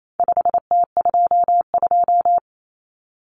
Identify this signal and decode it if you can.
Morse code, heard as a keyed tone, about 28 wpm, 700 Hz. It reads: 5T22
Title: Common Cut Numbers - 4-Digit 28wpm